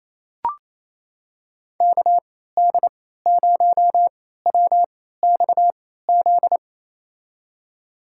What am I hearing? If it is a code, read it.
Morse code, heard as a keyed tone, about 28 wpm, spaced as for 16 wpm, 700 Hz. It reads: KD0WXZ
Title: USA Call Signs - 6 Characters - 3x Character Spacing - 28wpm